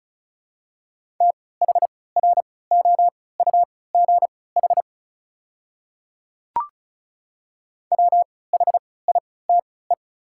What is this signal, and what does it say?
Morse code, heard as a keyed tone, about 36 wpm, spaced as for 20 wpm, 690 Hz. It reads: THROUGH WHITE
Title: 3x Character Spacing - Top 300 Words - 35wpm